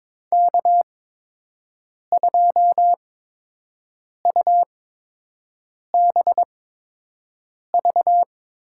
Morse code, heard as a keyed tone, about 22 wpm, spaced as for 6 wpm, 700 Hz. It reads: K2UBV
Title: USA Call Signs - 5 Characters - 8x Character Spacing - 22wpm